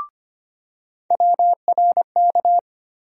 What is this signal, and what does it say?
Morse code, heard as a keyed tone, about 25 wpm, 700 Hz. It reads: WRK
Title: USA Call Sign Postfixes - 3 Letters 25wpm